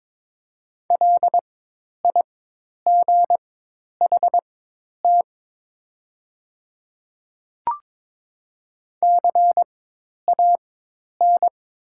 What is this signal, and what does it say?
Morse code, heard as a keyed tone, about 22 wpm, spaced as for 10 wpm, 700 Hz. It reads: LIGHT CAN
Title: 4x Character Spacing - Top 300 Words - 22wpm